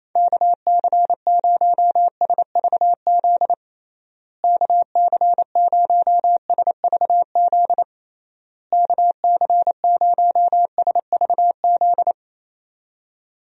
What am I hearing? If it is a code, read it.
Morse code, heard as a keyed tone, about 28 wpm, 700 Hz. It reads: KC0SVZ KC0SVZ KC0SVZ
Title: USA Call Signs - 6 Characters - Repeated 3x - 28wpm